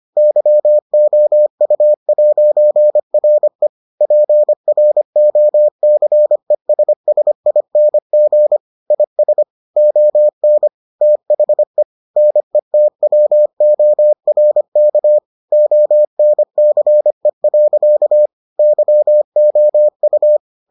Morse code, heard as a keyed tone, about 25 wpm, 600 Hz. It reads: YOU'RE PROCESSING IS ON THE NETWORK ONCE. YOU